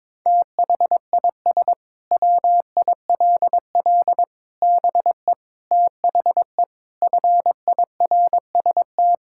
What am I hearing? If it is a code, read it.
Morse code, heard as a keyed tone, about 22 wpm, 700 Hz. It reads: THIS WILL BE THE FIRST
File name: practice / 22wpm / Sentences with 8-Word Phrases - 22wpm